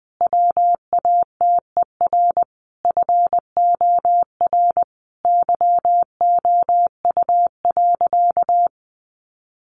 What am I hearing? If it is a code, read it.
Morse code, heard as a keyed tone, about 19 wpm, 700 Hz. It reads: WATER FOR YOU.
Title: Sentences from Top 1300 Words 20wpm